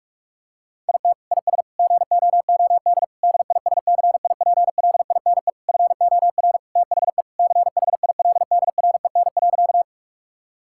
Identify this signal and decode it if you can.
Morse code, heard as a keyed tone, about 46 wpm, 700 Hz. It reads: IT IS GOOD DISCIPLINE FOR THE CHILDREN.
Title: Sentences from Top 1000 Words - Review Part 6 - 45wpm